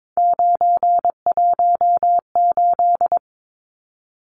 Morse code, heard as a keyed tone, about 22 wpm, 700 Hz. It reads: 918
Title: Numbers - 3-Digit 22wpm